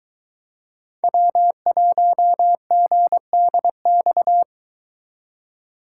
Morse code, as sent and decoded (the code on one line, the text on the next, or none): .-- .---- --. -.. -..-
W1GDX